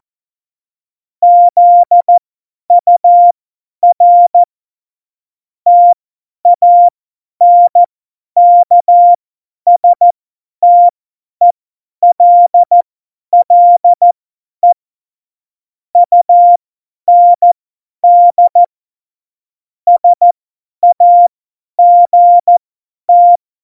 Morse code, heard as a keyed tone, about 14 wpm, spaced as for 10 wpm, 700 Hz. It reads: ZUR TANKSTELLE UND SAGT
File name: Morsekurs / Witze / Witz 12. Februar 2019, 50 F70